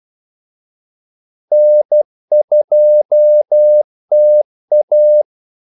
Morse code, heard as a keyed tone, about 12 wpm, 600 Hz. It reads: N2TA
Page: 2 ekstra ordmellemrum mellem kaldesignalerne